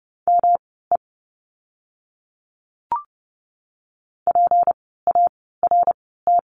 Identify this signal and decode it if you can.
Morse code, heard as a keyed tone, about 30 wpm, spaced as for 17 wpm, 700 Hz. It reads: ME PART
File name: course / 30:10wpm / 230 Lesson 37f - Letter x - Words 30wpm